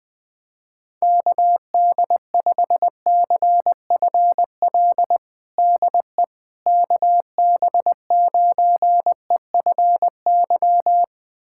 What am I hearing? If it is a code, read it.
Morse code, heard as a keyed tone, about 20 wpm, 700 Hz. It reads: KD5CFL DE KB9EFY
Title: USA Callsign DE Callsign 20wpm